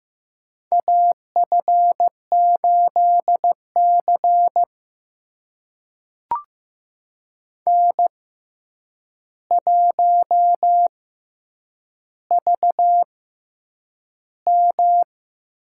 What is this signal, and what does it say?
Morse code, heard as a keyed tone, about 15 wpm, spaced as for 8 wpm, 700 Hz. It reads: AF8C N1VM
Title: USA Call Signs - 4 Characters - 6x Character Spacing - 15wpm